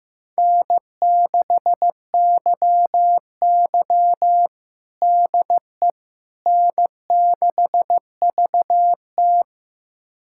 Morse code, heard as a keyed tone, about 15 wpm, 700 Hz. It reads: N6YY DE N6VT
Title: USA Callsign DE Callsign - 4 Characters - 15wpm